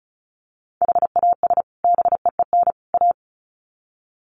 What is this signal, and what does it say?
Morse code, heard as a keyed tone, about 34 wpm, 700 Hz. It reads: HAS BEEN A
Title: Sets of 3 Words 35wpm